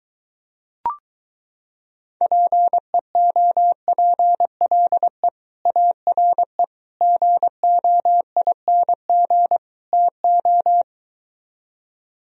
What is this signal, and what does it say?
Morse code, heard as a keyed tone, about 23 wpm, 700 Hz. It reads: PEOPLE ARE GOING TO